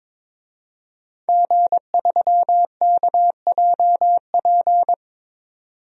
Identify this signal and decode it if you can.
Morse code, heard as a keyed tone, about 22 wpm, 700 Hz. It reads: G3KJP
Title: DXCC-100 Call Signs - 5 Characters - 22wpm